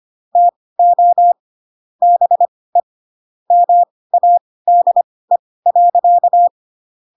Morse code, heard as a keyed tone, about 25 wpm, spaced as for 18 wpm, 700 Hz. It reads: TO BE MADE.
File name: Pinocchio ch3 12 25 ik1hge